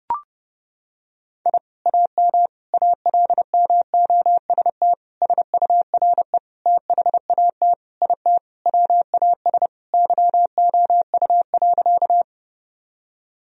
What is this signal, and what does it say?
Morse code, heard as a keyed tone, about 30 wpm, 700 Hz. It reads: I AM ALMOST SURE THAT IT WAS YOU.
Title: Sentences from Top 1000 Words - Review Part 4 - 30wpm